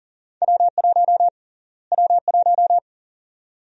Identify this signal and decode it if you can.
Morse code, heard as a keyed tone, about 38 wpm, 700 Hz. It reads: W1 W1
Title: USA Call Sign Prefixes - Repeated 2x - 40wpm